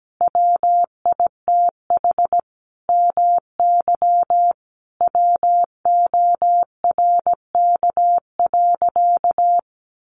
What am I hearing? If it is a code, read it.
Morse code, heard as a keyed tone, about 17 wpm, 700 Hz. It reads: WITH MY WORK.
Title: Sentences from Top 300 Words 17wpm